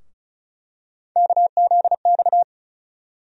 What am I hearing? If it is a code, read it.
Morse code, heard as a keyed tone, about 36 wpm, 700 Hz. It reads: KZX
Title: USA Call Sign Postfixes - 3 Letters 35wpm